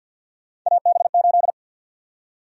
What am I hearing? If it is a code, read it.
Morse code, heard as a keyed tone, about 53 wpm, 700 Hz. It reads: AB8
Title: USA Call Sign Prefixes - 50wpm